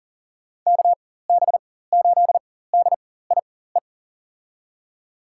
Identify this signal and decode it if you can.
Morse code, heard as a keyed tone, about 40 wpm, spaced as for 19 wpm, 700 Hz. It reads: KB8DIE
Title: USA Call Signs - 6 Characters - 4x Character Spacing - 40wpm